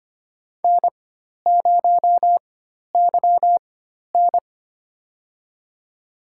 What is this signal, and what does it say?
Morse code, heard as a keyed tone, about 25 wpm, spaced as for 12 wpm, 700 Hz. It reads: N0YN
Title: USA Call Signs - 4 Characters - 4x Character Spacing - 25wpm